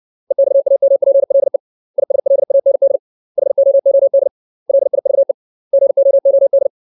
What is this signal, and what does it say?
Morse code, heard as a keyed tone, about 60 wpm, 550 Hz. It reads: EXAMPLE ISLAND HOOD LIVE GOOD
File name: 632 words / 632 Words 60WPM